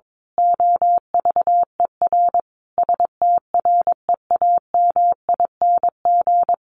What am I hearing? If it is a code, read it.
Morse code, heard as a keyed tone, about 22 wpm, 700 Hz. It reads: OVER STREAMING
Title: A Sound Of Thunder 22wpm